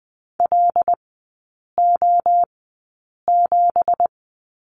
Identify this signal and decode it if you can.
Morse code, heard as a keyed tone, about 20 wpm, spaced as for 8 wpm, 690 Hz. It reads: LO7
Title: 166 Lesson 27e - Number 7 - Three Characters 20wpm